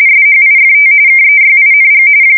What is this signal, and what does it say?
RTTY, about 45.45 baud. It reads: SHIFT KEYING
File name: fsk170 / teletype-RTTY